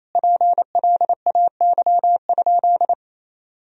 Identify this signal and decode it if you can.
Morse code, heard as a keyed tone, about 28 wpm, 700 Hz. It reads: PLAY?
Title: Sentences from Top 400 Words and Abbreviations 28wpm